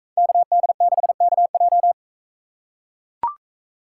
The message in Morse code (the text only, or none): KD6KJ